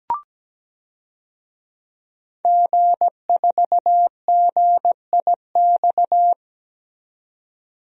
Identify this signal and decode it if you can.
Morse code, heard as a keyed tone, about 17 wpm, 700 Hz. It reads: G4GIX
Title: DXCC-100 Call Signs - 5 Characters - 17wpm